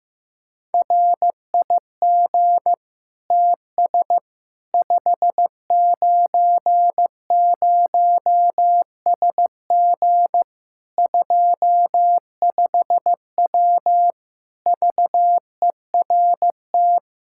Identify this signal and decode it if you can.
Morse code, heard as a keyed tone, about 15 wpm, 700 Hz. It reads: RIG TS 590SG 25W VERT
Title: Practice QSO - Exchange 2 - Info - 15wpm